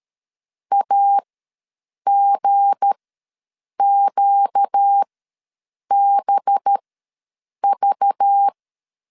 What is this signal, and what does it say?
Morse code, heard as a keyed tone, about 13 wpm, spaced as for 7 wpm, 800 Hz. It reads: AGQBV
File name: LEZIONI / 7 GF9Z / 7 G ws 12wpm 3140